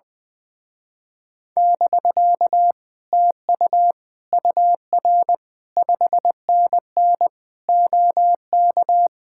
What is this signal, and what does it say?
Morse code, heard as a keyed tone, about 20 wpm, 700 Hz. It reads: <BK> TU UR 5NN OK
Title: Speed-Racing Common QSO Phrases - 20wpm